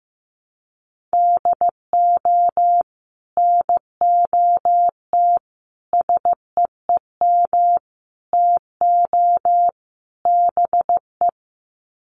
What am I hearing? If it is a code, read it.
Morse code, heard as a keyed tone, about 15 wpm, 700 Hz. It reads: DO NOT SEEM TO BE
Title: Sets of 5 Words Encore2 15wpm